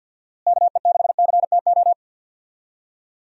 Morse code, heard as a keyed tone, about 48 wpm, 700 Hz. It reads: KE6CTO
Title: USA Call Signs - 6 Characters - Repeated 2x - 50wpm